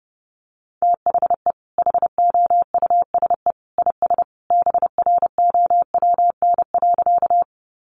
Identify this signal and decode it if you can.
Morse code, heard as a keyed tone, about 30 wpm, 700 Hz. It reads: THE HOUSE IS BROWN.
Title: Sentences from Top 1200 Words 30wpm